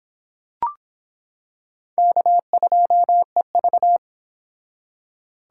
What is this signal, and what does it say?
Morse code, heard as a keyed tone, about 25 wpm, 700 Hz. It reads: K2EV